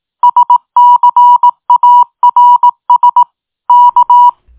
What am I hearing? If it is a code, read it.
Morse code, heard as a keyed tone, about 18 wpm, 970 Hz. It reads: SCARS K
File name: tin-upgrade / QSO 2025 01 16